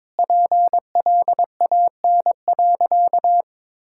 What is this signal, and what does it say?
Morse code, heard as a keyed tone, about 22 wpm, 700 Hz. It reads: PLAN.
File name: practice / 22wpm / Sentences with 2-Word Phrases - 22wpm